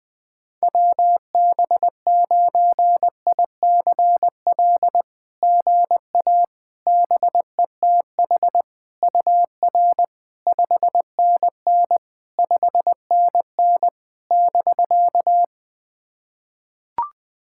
Morse code, heard as a keyed tone, about 20 wpm, 700 Hz. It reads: WB9ICL GA BETH UR 5NN 5NN <BK>
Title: SOTA QSO Practice 20wpm